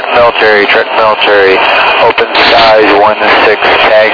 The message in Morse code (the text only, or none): none